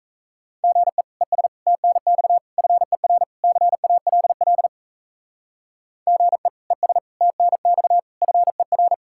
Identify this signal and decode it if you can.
Morse code, heard as a keyed tone, about 34 wpm, 700 Hz. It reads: GE ES TNX FER CALL GE ES TNX FER
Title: Speed-Racing Common QSO Phrases - 28wpm